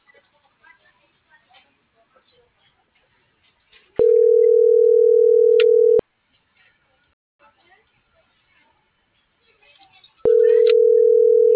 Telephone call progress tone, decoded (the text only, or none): ringback tone